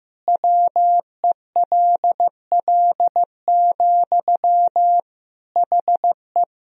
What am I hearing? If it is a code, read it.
Morse code, heard as a keyed tone, about 15 wpm, 700 Hz. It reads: WELL, HE
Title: Sentences from Top 500 Words and Abbreviations 15wpm